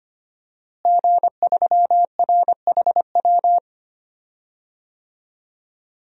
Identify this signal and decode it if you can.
Morse code, heard as a keyed tone, about 25 wpm, 700 Hz. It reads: G3RHW
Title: DXCC-100 Call Signs - 5 Characters - Repeated 3x - 25wpm